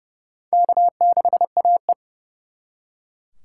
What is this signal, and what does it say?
Morse code, heard as a keyed tone, about 37 wpm, 700 Hz. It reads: K6AE
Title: Speed Racing - USA 1x2 Call Signs 30wpm